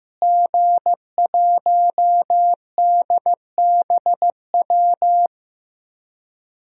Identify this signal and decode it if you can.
Morse code, heard as a keyed tone, about 15 wpm, 700 Hz. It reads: G1DBW
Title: DXCC-100 Call Signs - 5 Characters - Repeated 3x - 15wpm